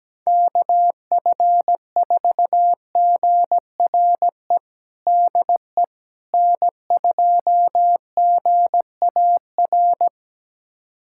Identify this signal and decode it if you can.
Morse code, heard as a keyed tone, about 17 wpm, 700 Hz. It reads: KF4GRE DE N2GAR
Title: USA Callsign DE Callsign 17wpm